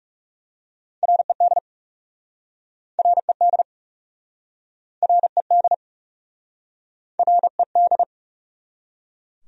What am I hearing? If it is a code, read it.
Morse code, heard as a keyed tone, about 40 wpm, 700 Hz. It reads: RED RED RED RED